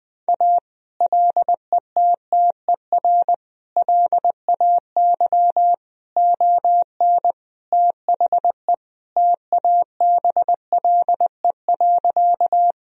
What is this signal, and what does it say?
Morse code, heard as a keyed tone, about 20 wpm, 700 Hz. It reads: A LETTER LAY ON THE TABLE.
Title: Sentences from Top 1000 Words - Review Part 2 - 20wpm